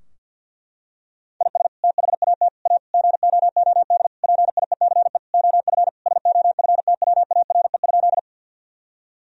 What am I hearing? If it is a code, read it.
Morse code, heard as a keyed tone, about 48 wpm, 700 Hz. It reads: IS THAT A GOOD PIECE OF SOFTWARE?
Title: Sentences from Top 1000 Words - Review Part 3 - 50wpm